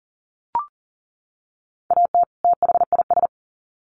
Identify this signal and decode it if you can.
Morse code, heard as a keyed tone, about 40 wpm, 700 Hz. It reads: AT THIS
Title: Sets of 2 Words 40wpm